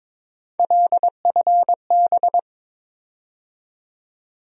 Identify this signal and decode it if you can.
Morse code, heard as a keyed tone, about 22 wpm, 700 Hz. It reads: LFB
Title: USA Call Sign Postfixes - 3 Letters 22wpm